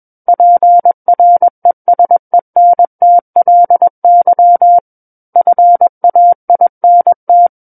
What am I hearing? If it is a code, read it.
Morse code, heard as a keyed tone, about 21 wpm, 700 Hz. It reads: PRESENTLY FAINT
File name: Book One / 12-Chapter 12 - 21 WPM